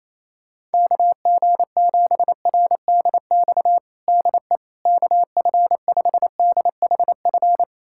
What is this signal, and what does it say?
Morse code, heard as a keyed tone, about 28 wpm, 700 Hz. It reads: KG7RDX DE KF5DHF
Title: USA Callsign DE Callsign - 6 Characters - 28wpm